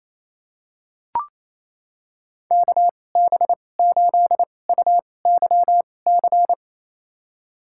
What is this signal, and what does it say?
Morse code, heard as a keyed tone, about 28 wpm, spaced as for 20 wpm, 700 Hz. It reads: KB8UYC
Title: USA Call Signs - 6 Characters - 2x Character Spacing - 28wpm